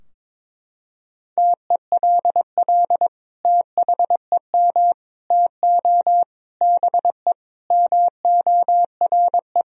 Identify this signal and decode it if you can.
Morse code, heard as a keyed tone, about 22 wpm, 700 Hz. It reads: TELL THEM TO BE MORE
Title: Sentences from Top 1000 Words - Review Part 3 - 22wpm